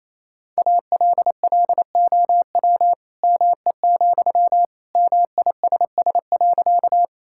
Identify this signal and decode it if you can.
Morse code, heard as a keyed tone, about 27 wpm, 700 Hz. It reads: ALLOW ME, MISS.
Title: Sentences from Top 1000 Words - Review Part 5 - 28wpm